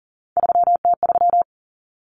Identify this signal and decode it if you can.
Morse code, heard as a keyed tone, about 40 wpm, 700 Hz. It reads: 3T3
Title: Common Cut Numbers - 3-Digit 40wpm